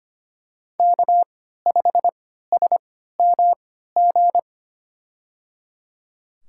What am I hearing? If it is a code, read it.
Morse code, heard as a keyed tone, about 25 wpm, spaced as for 14 wpm, 700 Hz. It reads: K5SMG
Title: USA Call Signs - 3x Character Spacing - 25wpm